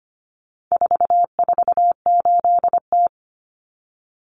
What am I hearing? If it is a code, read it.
Morse code, heard as a keyed tone, about 25 wpm, 700 Hz. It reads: 448T
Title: Common Cut Numbers - 4-Digit 25wpm